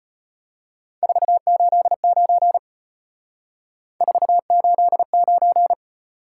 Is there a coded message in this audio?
Yes: Morse code from 1.0 to 5.7 s, heard as a keyed tone, about 35 wpm, 700 Hz. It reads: 489 489